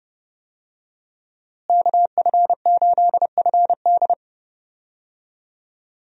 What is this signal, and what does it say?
Morse code, heard as a keyed tone, about 30 wpm, 700 Hz. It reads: KF8FD
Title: USA Call Signs - 5 Characters - Repeated 3x - 30wpm